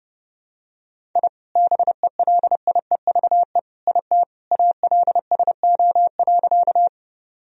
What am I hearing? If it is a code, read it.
Morse code, heard as a keyed tone, about 30 wpm, 700 Hz. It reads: I BELIEVE IT ALSO.
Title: Sentences from Top 1000 Words - Review Part 1 - 30wpm